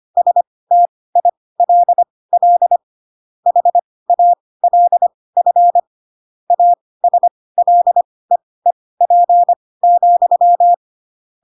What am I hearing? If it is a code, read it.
Morse code, heard as a keyed tone, about 25 wpm, spaced as for 18 wpm, 700 Hz. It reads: STILL HALF ASLEEP,